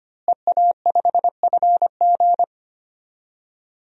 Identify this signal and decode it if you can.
Morse code, heard as a keyed tone, about 25 wpm, 700 Hz. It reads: EA5FG
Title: DXCC-100 Call Signs - 5 Characters - Repeated 2x - 25wpm